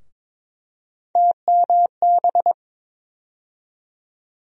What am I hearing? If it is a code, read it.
Morse code, heard as a keyed tone, about 22 wpm, 700 Hz. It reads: TMB
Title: USA Call Sign Postfixes - 3 Letters 22wpm